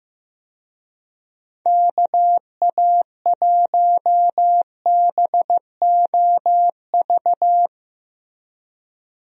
Morse code, heard as a keyed tone, about 14 wpm, 700 Hz. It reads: KA1BOV
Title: USA Call Signs - 6 Characters - Repeated 3x - 15wpm